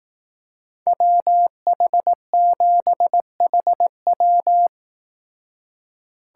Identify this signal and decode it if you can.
Morse code, heard as a keyed tone, about 18 wpm, 700 Hz. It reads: WH7HW
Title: Speed Racing - USA 2x2 Call Signs 15wpm